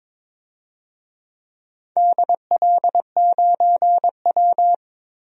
Morse code, heard as a keyed tone, about 23 wpm, 700 Hz. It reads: DL9W
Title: DXCC-100 Call Signs - 4 Characters - 22wpm